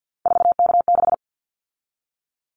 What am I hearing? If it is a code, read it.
Morse code, heard as a keyed tone, about 50 wpm, 700 Hz. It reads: 4K6
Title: Callsign Prefixes for Top 100 DXCC - 50wpm